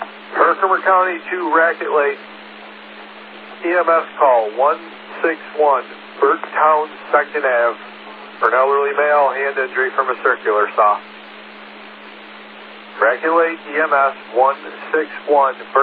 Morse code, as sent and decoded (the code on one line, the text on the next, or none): none